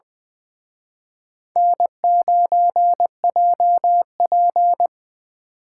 Morse code, heard as a keyed tone, about 23 wpm, 700 Hz. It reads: N9JP
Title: Speed Racing - USA Call Signs 20wpm